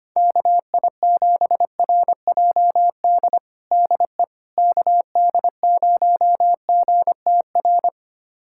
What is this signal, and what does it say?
Morse code, heard as a keyed tone, about 25 wpm, 700 Hz. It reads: KI7RJD DE KD0GTR
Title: USA Callsign DE Callsign - 6 Characters - 25wpm